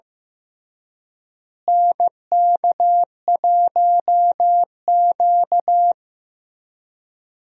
Morse code, heard as a keyed tone, about 15 wpm, 700 Hz. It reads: NK1Q
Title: USA Call Signs - 4 Characters - Repeated 3x - 15wpm